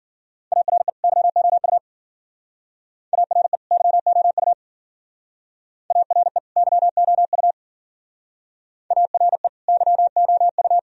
Morse code, heard as a keyed tone, about 53 wpm, 700 Hz. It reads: ARE YOU ARE YOU ARE YOU ARE YOU